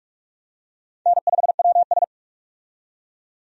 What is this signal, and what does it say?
Morse code, heard as a keyed tone, about 46 wpm, 700 Hz. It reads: N5WS
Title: USA Call Signs - 4 Characters - 45wpm